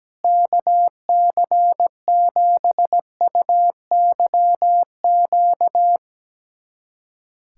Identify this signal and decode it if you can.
Morse code, heard as a keyed tone, about 17 wpm, 700 Hz. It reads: KC7UYQ
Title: USA Callsign DE Callsign - 6 Characters - 17wpm